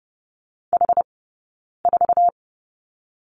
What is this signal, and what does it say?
Morse code, heard as a keyed tone, about 30 wpm, spaced as for 9 wpm, 700 Hz. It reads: H4
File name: Lesson 18d - Letter m - Two Characters 30wpm